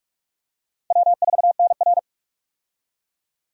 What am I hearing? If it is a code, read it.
Morse code, heard as a keyed tone, about 44 wpm, 700 Hz. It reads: W4NR
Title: USA Call Signs - 4 Characters - 45wpm